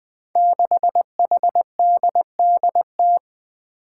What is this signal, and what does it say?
Morse code, heard as a keyed tone, about 20 wpm, 700 Hz. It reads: 6HDDT
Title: Five Challenging Letter-Number Combinations 20wpm